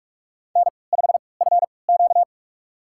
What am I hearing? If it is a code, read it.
Morse code, heard as a keyed tone, about 44 wpm, spaced as for 25 wpm, 690 Hz. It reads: N5FQ
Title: USA Call Signs - 4 Characters - 3x Character Spacing - 45wpm